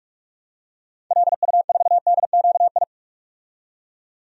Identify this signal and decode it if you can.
Morse code, heard as a keyed tone, about 47 wpm, 700 Hz. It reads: LU4DQI
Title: DXCC-100 Call Signs - 6 Characters - 45wpm